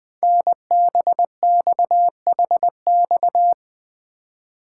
Morse code, heard as a keyed tone, about 20 wpm, 700 Hz. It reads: NBXHX
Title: Code Groups - 5 Letters - 20wpm